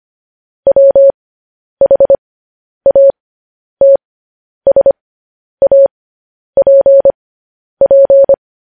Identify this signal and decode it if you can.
Morse code, heard as a keyed tone, about 25 wpm, spaced as for 10 wpm, 550 Hz. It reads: WHATSAPP